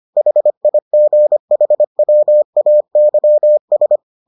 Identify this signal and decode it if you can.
Morse code, heard as a keyed tone, about 25 wpm, 600 Hz. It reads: HIGHWAYS